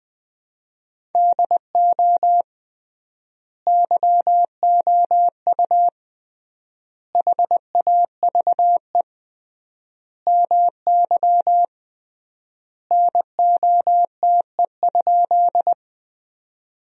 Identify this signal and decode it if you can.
Morse code, heard as a keyed tone, about 20 wpm, 700 Hz. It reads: DO YOU HAVE MY NOTE?
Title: 3x Word-Spacing - Sentences from Top 500 Words - 20wpm